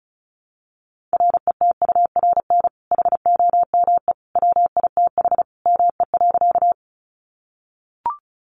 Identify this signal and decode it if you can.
Morse code, heard as a keyed tone, about 34 wpm, 700 Hz. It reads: RETURN HOME WITH ME.
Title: Sentences from Top 800 Words 35wpm